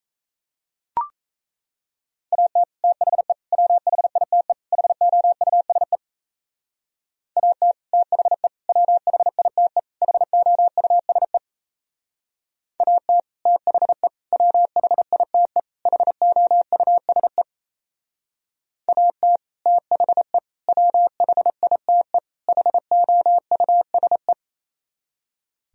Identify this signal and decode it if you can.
Morse code, heard as a keyed tone, about 34 wpm, 700 Hz. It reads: AT THE WHITE HOUSE AT THE WHITE HOUSE AT THE WHITE HOUSE AT THE WHITE HOUSE
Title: Speed-Racing - Sets of 4 Words 28wpm